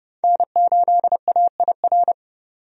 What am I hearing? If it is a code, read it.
Morse code, heard as a keyed tone, about 30 wpm, 700 Hz. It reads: N8AIR
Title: USA Call Signs - 5 Characters - 30wpm